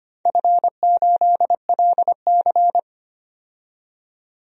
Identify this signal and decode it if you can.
Morse code, heard as a keyed tone, about 25 wpm, 700 Hz. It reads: F8LC
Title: DXCC-100 Call Signs - 4 Characters - 25wpm